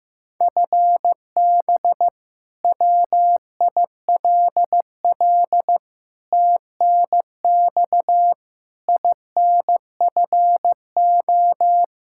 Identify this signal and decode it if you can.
Morse code, heard as a keyed tone, about 15 wpm, 700 Hz. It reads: FB WILL TNX INFO
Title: Practice QSO - Exchange 3 - More Info - 15wpm